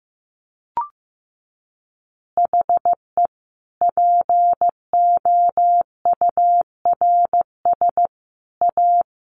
Morse code, heard as a keyed tone, about 15 wpm, 700 Hz. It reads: HE POURS A
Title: Additional Easy Phrases 15wpm